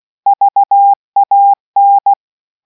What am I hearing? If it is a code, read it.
Morse code, heard as a keyed tone, about 16 wpm, 800 Hz. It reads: VAN